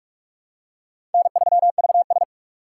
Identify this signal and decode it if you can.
Morse code, heard as a keyed tone, about 44 wpm, 690 Hz. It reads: N3VS
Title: USA Call Signs - 4 Characters - 4x Character Spacing - 45wpm